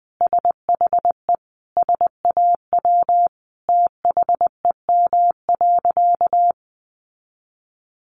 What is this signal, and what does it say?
Morse code, heard as a keyed tone, about 19 wpm, 700 Hz. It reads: SHE SAW THEM.
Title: Sentences from Top 1300 Words 20wpm